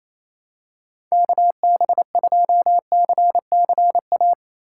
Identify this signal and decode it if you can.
Morse code, heard as a keyed tone, about 28 wpm, 700 Hz. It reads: KB2CCA
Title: USA Call Signs - 6 Characters - 2x Character Spacing - 28wpm